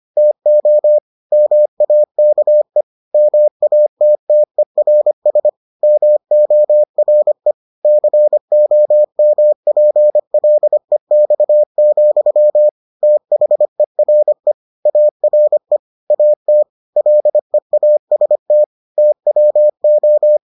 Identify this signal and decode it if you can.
Morse code, heard as a keyed tone, about 25 wpm, 600 Hz. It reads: TO MAKE MATTERS MORE COMPLEX, THERE ARE AT LEAST TWO